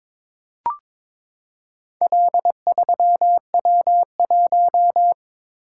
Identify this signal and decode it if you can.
Morse code, heard as a keyed tone, about 22 wpm, 700 Hz. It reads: L3W1
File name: practice / 22wpm / Four Letter-Number Combinations - 22wpm